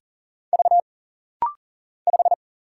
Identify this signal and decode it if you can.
Morse code, heard as a keyed tone, about 40 wpm, 700 Hz. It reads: V 5
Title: Single Letter-Number - Rapid-Fire 40wpm